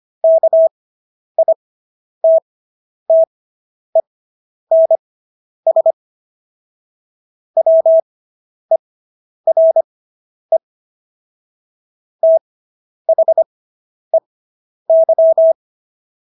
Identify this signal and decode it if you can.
Morse code, heard as a keyed tone, about 25 wpm, spaced as for 10 wpm, 650 Hz. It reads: KITTENS WERE THEY